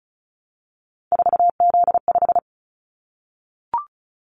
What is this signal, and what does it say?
Morse code, heard as a keyed tone, about 36 wpm, 700 Hz. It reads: 4Z5
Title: Callsign Prefixes for Top 50 DXCC - 35wpm